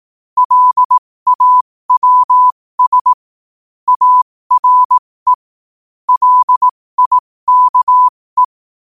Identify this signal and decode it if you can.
Morse code, heard as a keyed tone, about 18 wpm, spaced as for 15 wpm, 1000 Hz. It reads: LAWS ARE LIKE
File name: qotdfiles / 15wpm